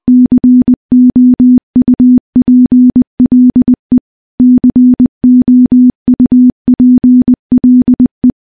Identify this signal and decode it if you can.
Morse code, heard as a keyed tone, about 20 wpm, 250 Hz. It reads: COUPLE COUPLE
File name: morse-training-20WPM